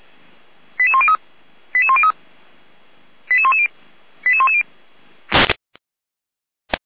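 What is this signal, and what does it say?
Five-tone selective call (ZVEI), heard as quick runs of five tones, tones of about 70 ms. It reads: 88103 88103 88110 88110
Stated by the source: ZVEI five-tone call 88109 received on SDR.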